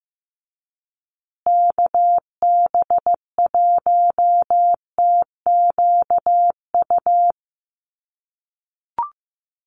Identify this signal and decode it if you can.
Morse code, heard as a keyed tone, about 15 wpm, 700 Hz. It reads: KB1TQU
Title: USA Call Signs Repeated 2x 15wpm